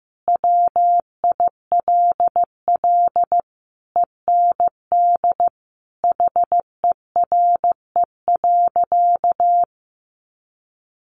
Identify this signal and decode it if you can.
Morse code, heard as a keyed tone, about 15 wpm, 700 Hz. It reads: WILL END HERE.
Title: Sentences from Top 1200 Words 15wpm